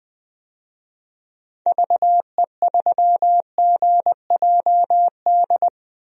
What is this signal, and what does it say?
Morse code, heard as a keyed tone, about 20 wpm, 700 Hz. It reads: VE3GJD
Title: DXCC-100 Call Signs - 6 Characters - Repeated 2x - 20wpm